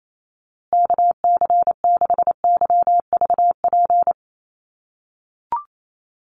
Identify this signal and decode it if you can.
Morse code, heard as a keyed tone, about 28 wpm, 700 Hz. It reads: KC6YVP
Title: USA Call Signs Repeated 3x 28wpm